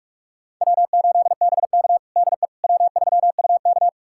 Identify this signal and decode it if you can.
Morse code, heard as a keyed tone, about 46 wpm, 700 Hz. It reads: W8BK DE W3UK
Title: USA Callsign DE Callsign - 4 Characters - 45wpm